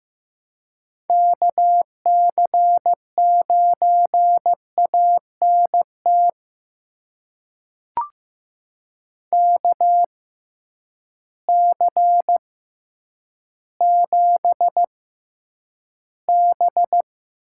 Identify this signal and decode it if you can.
Morse code, heard as a keyed tone, about 15 wpm, spaced as for 9 wpm, 700 Hz. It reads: KC9ANT KC7B
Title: USA Call Signs - 6 Characters - 6x Character Spacing - 15wpm